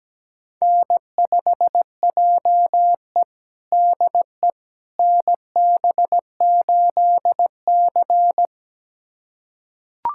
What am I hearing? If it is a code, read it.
Morse code, heard as a keyed tone, about 17 wpm, 700 Hz. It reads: N5JE DE NB8C
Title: USA Callsign DE Callsign - 4 Characters - 17wpm